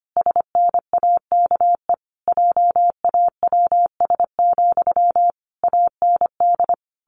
Morse code, heard as a keyed tone, about 25 wpm, 700 Hz. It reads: SNAKE JAWS, AND
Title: A Sound Of Thunder 25wpm